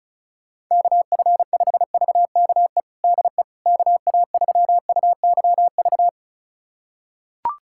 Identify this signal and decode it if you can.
Morse code, heard as a keyed tone, about 34 wpm, 700 Hz. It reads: KF5VKE DE KA3UYV
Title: USA Callsign DE Callsign - 6 Characters - 35wpm